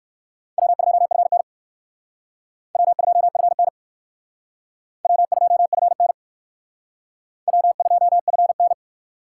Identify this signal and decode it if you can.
Morse code, heard as a keyed tone, about 64 wpm, 700 Hz. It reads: W2FN W2FN W2FN W2FN